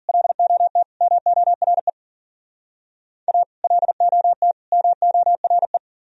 Morse code, heard as a keyed tone, about 47 wpm, 700 Hz. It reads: LOT MORE A LOT MORE
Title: Speed-Racing - Sets of 3 Words 40wpm